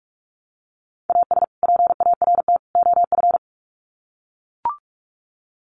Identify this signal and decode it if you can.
Morse code, heard as a keyed tone, about 44 wpm, 700 Hz. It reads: AS PART OF
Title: Sets of 3 Words 45wpm